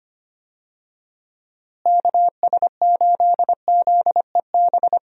Morse code, heard as a keyed tone, about 25 wpm, 700 Hz. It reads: KS8ZEB
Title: USA Call Signs - 6 Characters - Repeated 3x - 25wpm